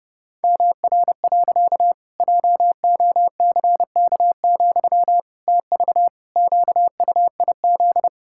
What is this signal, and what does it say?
Morse code, heard as a keyed tone, about 30 wpm, 700 Hz. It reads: MR. JOCK, TV QUIZ
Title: Pangram Challenge v2 - 30wpm